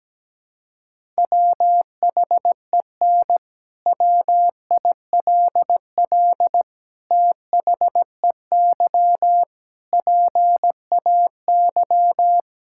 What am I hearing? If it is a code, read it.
Morse code, heard as a keyed tone, about 17 wpm, 700 Hz. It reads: WHEN WILL THEY PAY
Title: Sentences from Top 1000 Words - Review Part 5 - 17wpm